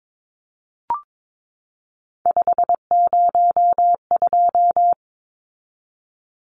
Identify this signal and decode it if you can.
Morse code, heard as a keyed tone, about 22 wpm, 700 Hz. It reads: 502
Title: Numbers - 3-Digit 22wpm